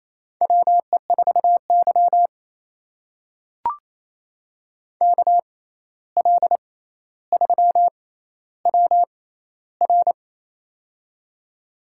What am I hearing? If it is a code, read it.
Morse code, heard as a keyed tone, about 27 wpm, spaced as for 13 wpm, 700 Hz. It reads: WE4Y KL3WR
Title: USA Call Signs - 6x Character Spacing - 28wpm